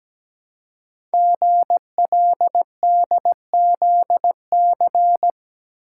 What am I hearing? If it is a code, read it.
Morse code, heard as a keyed tone, about 17 wpm, 700 Hz. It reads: GLDZC
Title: Code Groups - 5 Letters - 17wpm